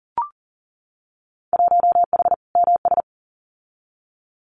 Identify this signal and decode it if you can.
Morse code, heard as a keyed tone, about 40 wpm, 700 Hz. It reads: 1H MS
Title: Winter Field Day Exchanges 40wpm